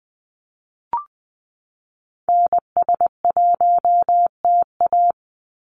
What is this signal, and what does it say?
Morse code, heard as a keyed tone, about 20 wpm, 700 Hz. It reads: NS1TA
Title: NA SOTA Activator Callsigns 20wpm